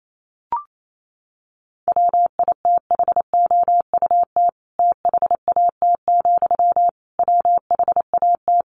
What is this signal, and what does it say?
Morse code, heard as a keyed tone, about 28 wpm, 700 Hz. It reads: WITHOUT THAT, WHAT
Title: Sentences from Top 600 Words 28wpm